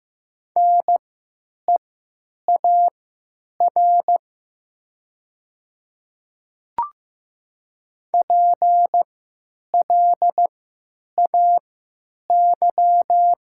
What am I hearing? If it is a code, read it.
Morse code, heard as a keyed tone, about 15 wpm, spaced as for 8 wpm, 700 Hz. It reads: NEAR PLAY
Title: 3x Character Spacing - Top 200 Words - 15wpm